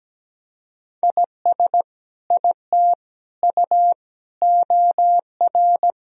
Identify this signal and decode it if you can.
Morse code, heard as a keyed tone, about 17 wpm, 700 Hz. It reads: IS IT U OR
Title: Sentences from Top 200 Words and Abbreviations 17wpm